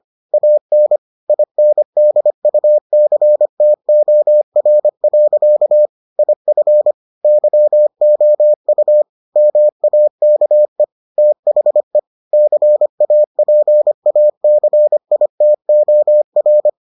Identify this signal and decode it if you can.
Morse code, heard as a keyed tone, about 25 wpm, 600 Hz. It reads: AN INDUCTOR. IF YOU MAKE THE CAPACITOR